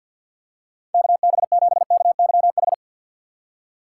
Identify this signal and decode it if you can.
Morse code, heard as a keyed tone, about 50 wpm, 700 Hz. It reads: KB7KYH
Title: USA Call Signs - 6x Character Spacing - 50wpm